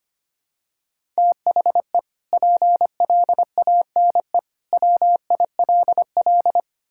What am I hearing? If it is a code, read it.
Morse code, heard as a keyed tone, about 25 wpm, 700 Hz. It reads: THE PLANE WILL
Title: Sentences from Top 1000 Words - Review Part 6 - 25wpm